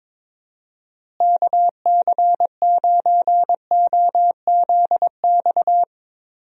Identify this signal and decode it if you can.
Morse code, heard as a keyed tone, about 21 wpm, 700 Hz. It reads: KC9OZX
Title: USA Call Signs - 6 Characters - Repeated 2x - 22wpm